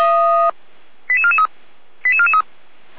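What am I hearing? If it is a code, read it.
Five-tone selective call (ZVEI), heard as quick runs of five tones, tones of about 70 ms. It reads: 88443 88443
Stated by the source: ZVEI five-tone call received on SDR.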